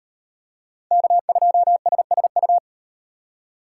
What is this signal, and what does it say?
Morse code, heard as a keyed tone, about 37 wpm, 700 Hz. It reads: K2SSU